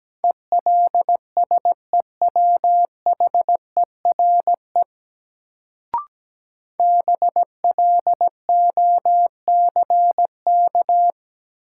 Morse code, heard as a keyed tone, about 17 wpm, 700 Hz. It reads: ELSEWHERE BLOCK
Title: Sentences from Top 1600 Words - New Word Drill Alt - 17wpm